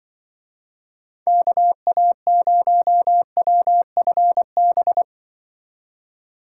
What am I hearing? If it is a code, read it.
Morse code, heard as a keyed tone, about 25 wpm, 700 Hz. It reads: KA0WFB
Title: Speed Racing - USA 2x3 Call Signs 20wpm